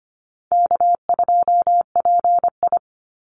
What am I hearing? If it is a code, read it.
Morse code, heard as a keyed tone, about 25 wpm, 700 Hz. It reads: K2PI
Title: Easy Call Signs - 25wpm